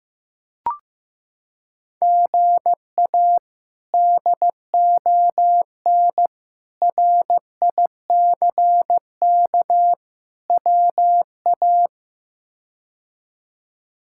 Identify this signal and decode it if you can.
Morse code, heard as a keyed tone, about 15 wpm, 700 Hz. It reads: GA DON RICK WA
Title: K1USN SST QSO Practice - 15wpm